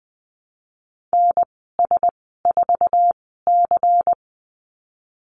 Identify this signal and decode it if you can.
Morse code, heard as a keyed tone, about 20 wpm, spaced as for 15 wpm, 700 Hz. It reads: NS4C
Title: 238 Lesson 38g - Letter q - Call Signs 20wpm